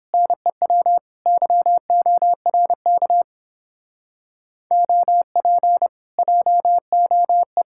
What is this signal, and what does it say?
Morse code, heard as a keyed tone, about 31 wpm, 700 Hz. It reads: NEW YORK OP JOE